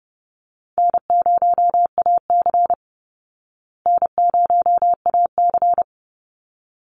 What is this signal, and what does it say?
Morse code, heard as a keyed tone, about 30 wpm, 700 Hz. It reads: N0AC N0AC
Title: Easy Call Signs Repeated 2x - 30wpm